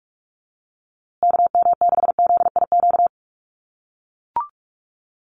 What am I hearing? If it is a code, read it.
Morse code, heard as a keyed tone, about 44 wpm, 700 Hz. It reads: KM6ZIQ
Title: USA Call Signs Repeated 2x 45wpm